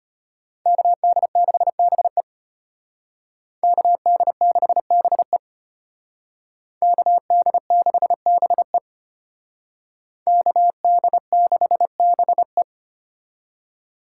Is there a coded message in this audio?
Yes: Morse code from 0.7 to 12.6 s, heard as a keyed tone, about 33 wpm, 700 Hz. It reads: KD6BE KD6BE KD6BE KD6BE